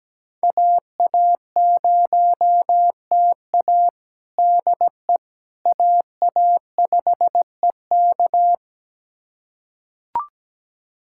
Morse code, heard as a keyed tone, about 17 wpm, 700 Hz. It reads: AA0TA DE AA5EK
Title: USA Callsign DE Callsign - 5 Characters - 17wpm